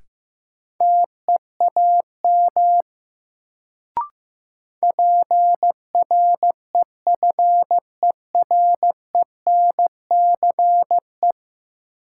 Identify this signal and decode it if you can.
Morse code, heard as a keyed tone, about 15 wpm, 700 Hz. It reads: TEAM PREFERENCE
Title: Sentences from Top 1500 Words - New Word Drill - 15wpm